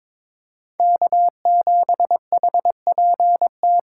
Morse code, heard as a keyed tone, about 22 wpm, 700 Hz. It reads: K7HPT
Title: K1USN SST QSO Practice - 22wpm